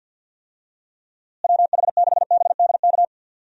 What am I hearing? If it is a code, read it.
Morse code, heard as a keyed tone, about 53 wpm, 700 Hz. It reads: WH6BDK
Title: USA Call Signs - 6 Characters - Repeated 2x - 50wpm